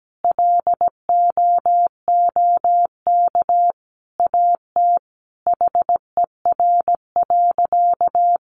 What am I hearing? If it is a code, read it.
Morse code, heard as a keyed tone, about 17 wpm, 700 Hz. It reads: LOOK AT HER.
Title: Sentences from Top 200 Words 17wpm